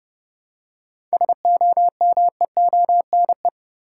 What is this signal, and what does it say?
Morse code, heard as a keyed tone, about 30 wpm, 700 Hz. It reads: SOMEONE
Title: Seven Letter Words - 30wpm